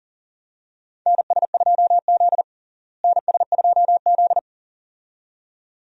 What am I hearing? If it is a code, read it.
Morse code, heard as a keyed tone, about 40 wpm, 700 Hz. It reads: NS2Z NS2Z
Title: USA Call Signs - 4 Characters - Repeated 2x - 40wpm